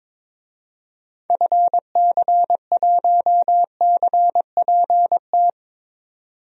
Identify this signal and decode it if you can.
Morse code, heard as a keyed tone, about 21 wpm, 700 Hz. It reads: FC1CPT
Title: DXCC-100 Call Signs - 6 Characters - 22wpm